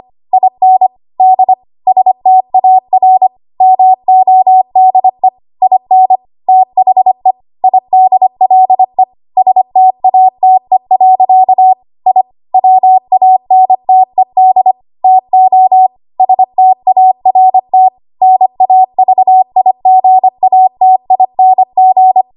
Morse code, heard as a keyed tone, about 25 wpm, 750 Hz. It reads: IN D STAR MODE IN THE IDLE STATE. I WANTED TO START NAVIGATING
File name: Archive / 25 WPM / 25WPM